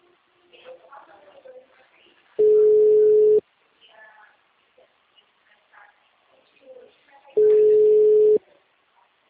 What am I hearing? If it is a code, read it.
Ringback tone (Europe, 425 Hz), with speech.
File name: out- -2004- - - .64